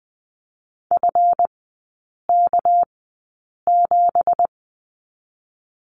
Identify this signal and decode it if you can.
Morse code, heard as a keyed tone, about 20 wpm, spaced as for 8 wpm, 700 Hz. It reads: FK7
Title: 218 Lesson 35e - Number 0 - Three Characters 20wpm